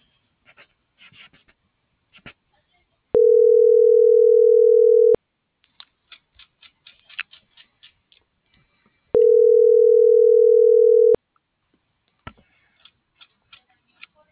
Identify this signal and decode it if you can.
Ringback tone (North America), 440 + 480 Hz.